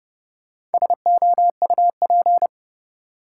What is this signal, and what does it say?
Morse code, heard as a keyed tone, about 30 wpm, 700 Hz. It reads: SOUP
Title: Sentences from Top 1600 Words - New Word Drill Alt - 30wpm